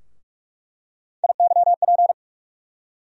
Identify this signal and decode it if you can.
Morse code, heard as a keyed tone, about 46 wpm, 700 Hz. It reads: IYP
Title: USA Call Sign Postfixes - 3 Letters 45wpm